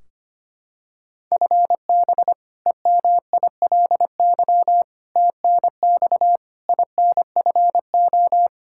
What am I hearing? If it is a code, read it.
Morse code, heard as a keyed tone, about 25 wpm, 700 Hz. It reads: FB EMILY TNX INFO
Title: Practice QSO - Exchange 3 - More Info - 25wpm